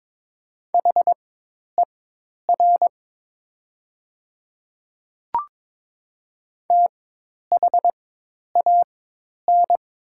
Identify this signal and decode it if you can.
Morse code, heard as a keyed tone, about 22 wpm, spaced as for 10 wpm, 700 Hz. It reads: HER THAN
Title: 4x Character Spacing - Top 100 Words - 22wpm